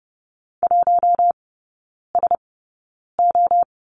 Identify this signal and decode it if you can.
Morse code, heard as a keyed tone, about 30 wpm, spaced as for 9 wpm, 700 Hz. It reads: 1SO
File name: 081 Lesson 14e - Number 2 - Three Characters 30wpm